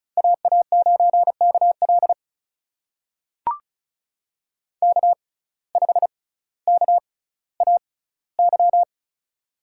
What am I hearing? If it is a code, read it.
Morse code, heard as a keyed tone, about 36 wpm, spaced as for 18 wpm, 700 Hz. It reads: AA9KL K5KAY
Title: USA Call Signs - 5 Characters - 6x Character Spacing - 35wpm